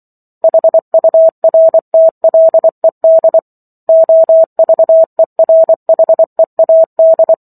Morse code, heard as a keyed tone, about 24 wpm, 650 Hz. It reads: HURTLED OVERHEAD